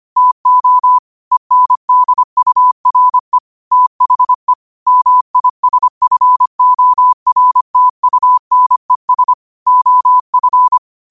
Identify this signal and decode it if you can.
Morse code, heard as a keyed tone, about 25 wpm, 1000 Hz. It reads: TO ENDURE THE MISFORTUNES OF